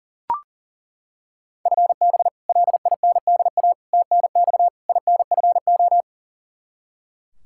Morse code, heard as a keyed tone, about 40 wpm, 700 Hz. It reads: FB LINDA TNX INFO
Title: Practice QSO - Exchange 3 - More Info - 40wpm